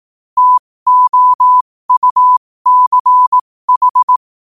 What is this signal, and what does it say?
Morse code, heard as a keyed tone, about 18 wpm, spaced as for 15 wpm, 1000 Hz. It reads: TOUCH
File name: qotdfiles / 15wpm